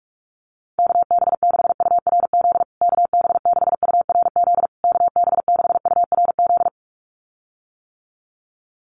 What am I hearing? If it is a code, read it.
Morse code, heard as a keyed tone, about 44 wpm, 690 Hz. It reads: KB6URZ KB6URZ KB6URZ
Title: USA Call Signs Repeated 3x 45wpm